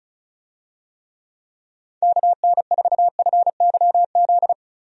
Morse code, heard as a keyed tone, about 34 wpm, 700 Hz. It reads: KN4FYZ
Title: USA Call Signs - 6 Characters - 35wpm